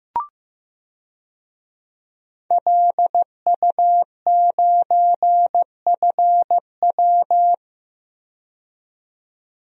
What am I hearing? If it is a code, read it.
Morse code, heard as a keyed tone, about 15 wpm, 700 Hz. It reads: LU9FW
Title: DXCC-100 Call Signs - 5 Characters - 15wpm